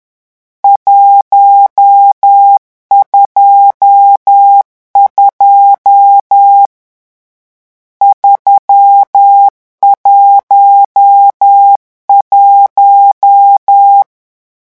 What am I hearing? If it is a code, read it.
Morse code, heard as a keyed tone, about 10 wpm, 800 Hz. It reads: 122 311
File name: morse mp3 / lecon10-vitesse 10